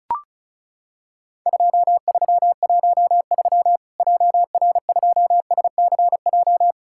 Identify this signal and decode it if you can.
Morse code, heard as a keyed tone, about 34 wpm, 700 Hz. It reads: 2313 JR2SCJ
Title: Worked All Europe DX Contest - QTC Exchange - 35wpm